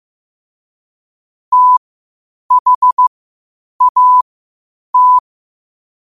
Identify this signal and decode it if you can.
Morse code, heard as a keyed tone, about 15 wpm, spaced as for 8 wpm, 1000 Hz. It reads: THAT
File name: qotdfiles / 8wpm